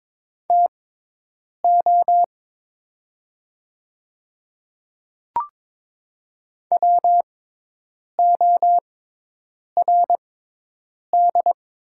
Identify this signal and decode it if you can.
Morse code, heard as a keyed tone, about 23 wpm, spaced as for 8 wpm, 700 Hz. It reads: TO WORD
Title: 6x Character Spacing - Top 200 Words - 22wpm